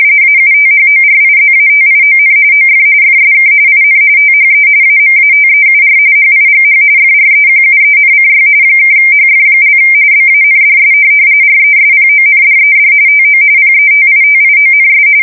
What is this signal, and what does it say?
RTTY, about 45.45 baud. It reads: AUTHORIZATION OF FSK IN THE AMATEUR HIGH FREQUENCY (HF) BANDS RESPONDED TO PETITIONS
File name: fsk170 / teletype-RTTY